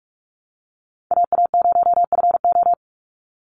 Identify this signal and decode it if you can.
Morse code, heard as a keyed tone, about 44 wpm, 700 Hz. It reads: AA0FO
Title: Call Signs Repeated 3x - 45wpm